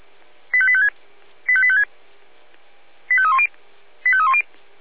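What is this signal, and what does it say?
Five-tone selective call (ZVEI), heard as quick runs of five tones, tones of about 70 ms. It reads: 85856 85856 85310 85310
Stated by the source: ZVEI five-tone selective call 85856 received with an SDR.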